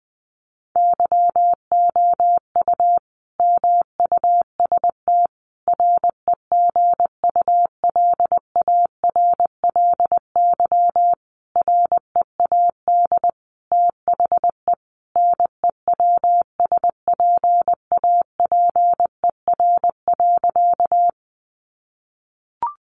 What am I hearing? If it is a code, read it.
Morse code, heard as a keyed tone, about 20 wpm, 700 Hz. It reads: YOU MUST REGULARLY READ THE NEWSPAPER.
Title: Sentences from Top 1400 Words 20wpm